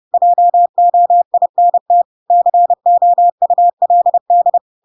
Morse code, heard as a keyed tone, about 30 wpm, 700 Hz. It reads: JOINT COULD
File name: Pinocchio / Pinocchio ch8 30 30 ik1hge